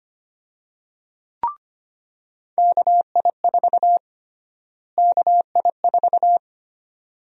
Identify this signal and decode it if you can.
Morse code, heard as a keyed tone, about 25 wpm, 700 Hz. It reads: KI4 KI4
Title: USA Call Sign Prefixes - Repeated 2x - 25wpm